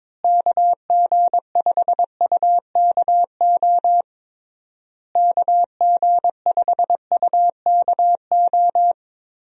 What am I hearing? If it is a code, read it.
Morse code, heard as a keyed tone, about 22 wpm, 700 Hz. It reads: KG5UKO KG5UKO
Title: USA Call Signs - 6 Characters - Repeated 3x - 22wpm